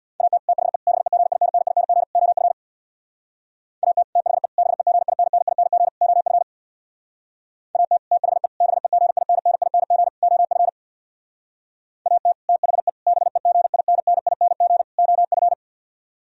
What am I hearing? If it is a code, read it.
Morse code, heard as a keyed tone, about 64 wpm, 700 Hz. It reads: AT THE BEGINNING OF AT THE BEGINNING OF AT THE BEGINNING OF AT THE BEGINNING OF